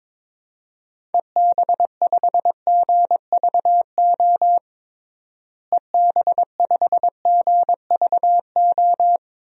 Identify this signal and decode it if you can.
Morse code, heard as a keyed tone, about 22 wpm, 700 Hz. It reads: EB5GVO EB5GVO
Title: DXCC-100 Call Signs - 6 Characters - Repeated 3x - 22wpm